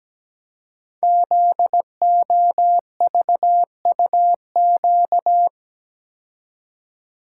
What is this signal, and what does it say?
Morse code, heard as a keyed tone, about 17 wpm, 700 Hz. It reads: ZOVUQ
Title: Code Groups - 5 Letters - 17wpm